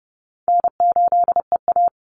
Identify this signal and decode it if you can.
Morse code, heard as a keyed tone, about 30 wpm, 700 Hz. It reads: N8EA
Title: Call Signs - 30wpm